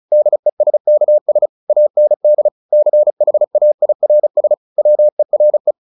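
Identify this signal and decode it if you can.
Morse code, heard as a keyed tone, about 34 wpm, 600 Hz. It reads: DESKS AND CHAIRS WERE
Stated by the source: Morse practice at 35 wpm from CW med Gnister, sending Ken Follett's novel Never in English.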